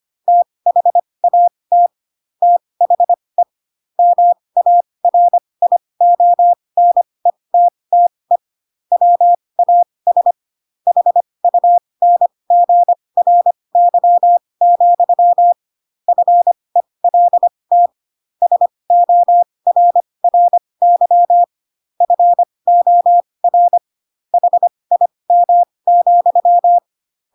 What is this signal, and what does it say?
Morse code, heard as a keyed tone, about 25 wpm, spaced as for 20 wpm, 700 Hz. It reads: THAT THE MARIONETTE WAS HUNGRY, FELT SORRY FOR HIM,
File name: Pinocchio / Pinocchio ch7 15 25 ik1hge